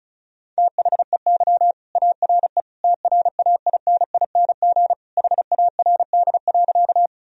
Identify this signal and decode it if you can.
Morse code, heard as a keyed tone, about 34 wpm, 700 Hz. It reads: THEY ARE TRAINING HARD.
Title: Sentences from Top 1000 Words - Review Part 4 - 35wpm